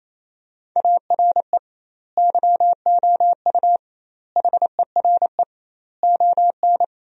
Morse code, heard as a keyed tone, about 28 wpm, 700 Hz. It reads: ARE YOU HERE ON
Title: 2x Word-Spacing - Sentences from Top 400 Words - 28wpm